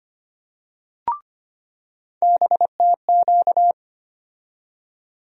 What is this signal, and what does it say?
Morse code, heard as a keyed tone, about 25 wpm, 700 Hz. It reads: BTQ
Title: USA Call Sign Postfixes - 3 Letters 25wpm